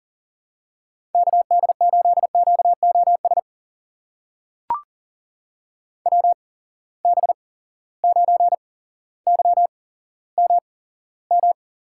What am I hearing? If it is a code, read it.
Morse code, heard as a keyed tone, about 39 wpm, spaced as for 17 wpm, 700 Hz. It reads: KD8QOS WB9YMM
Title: USA Call Signs - 6 Characters - 8x Character Spacing - 40wpm